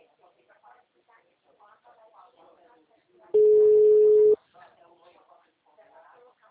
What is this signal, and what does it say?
Ringback tone (Europe, 425 Hz), with speech.
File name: out- -2007- - - .8122